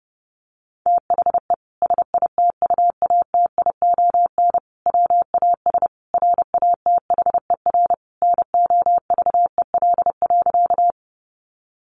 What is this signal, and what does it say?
Morse code, heard as a keyed tone, about 30 wpm, 700 Hz. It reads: THE SITUATION WAS RATHER NOVEL.
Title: Sentences from Top 1300 Words 30wpm